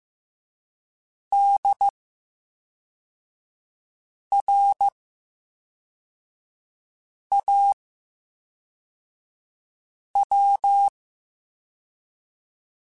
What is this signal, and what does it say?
Morse code, heard as a keyed tone, about 15 wpm, 800 Hz. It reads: D R A W
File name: Special INGs / Special INGs-mp3s / VerySlowToFaster Ing @10